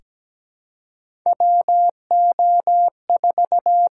Morse code, heard as a keyed tone, about 17 wpm, 700 Hz. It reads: WO4
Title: USA Call Signs - 6x Character Spacing - 17wpm